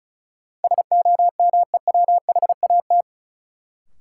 Morse code, heard as a keyed tone, about 36 wpm, 700 Hz. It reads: SOMEWHAT
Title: Sentences from Top 1000 Words - Review Part 1 - New Word Drill - 35wpm